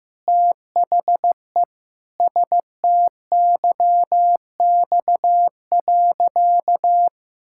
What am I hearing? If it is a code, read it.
Morse code, heard as a keyed tone, about 15 wpm, 700 Hz. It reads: THE STYX.
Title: Pangram Challenge v2 - 15wpm